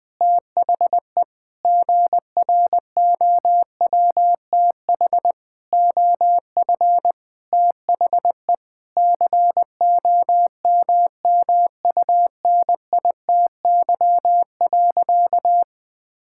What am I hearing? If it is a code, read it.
Morse code, heard as a keyed tone, about 20 wpm, 700 Hz. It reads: THE GROWTH OF THE COMMUNITY.
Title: Sentences from Top 1000 Words - Review Part 5 - 20wpm